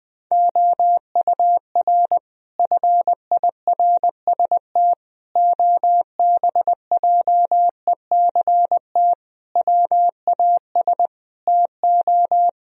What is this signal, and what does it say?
Morse code, heard as a keyed tone, about 20 wpm, 700 Hz. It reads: OUR FIRST OBJECT WAS TO
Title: Sentences from Top 1000 Words - Review Part 3 - 20wpm